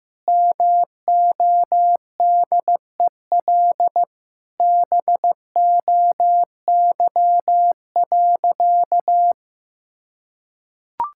Morse code, heard as a keyed tone, about 15 wpm, 700 Hz. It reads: MODEL BOY.
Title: Sentences from Top 1000 Words - Review Part 5 - 15wpm